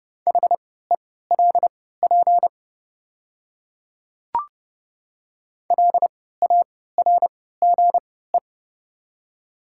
Morse code, heard as a keyed tone, about 30 wpm, spaced as for 17 wpm, 690 Hz. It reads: HELP LARGE
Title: 3x Character Spacing - Top 500 Words - 30wpm